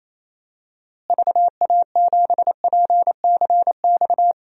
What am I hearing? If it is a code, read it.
Morse code, heard as a keyed tone, about 28 wpm, 700 Hz. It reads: VA7PCX
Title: DXCC-100 Call Signs - 6 Characters - Repeated 3x - 28wpm